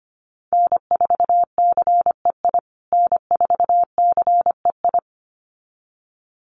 Morse code, heard as a keyed tone, about 25 wpm, 700 Hz. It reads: N4CEI N4CEI
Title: USA Call Signs Repeated 2x 25wpm